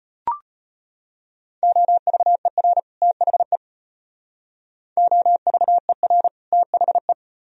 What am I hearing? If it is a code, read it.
Morse code, heard as a keyed tone, about 33 wpm, 700 Hz. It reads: OVER THE OVER THE